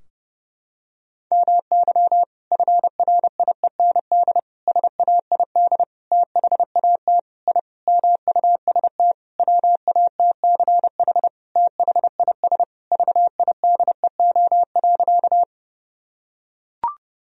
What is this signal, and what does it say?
Morse code, heard as a keyed tone, about 30 wpm, 700 Hz. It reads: MY FRIEND SAID THAT I MUST WATCH THIS VIDEO.
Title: Sentences from Top 1000 Words - Review Part 4 - 30wpm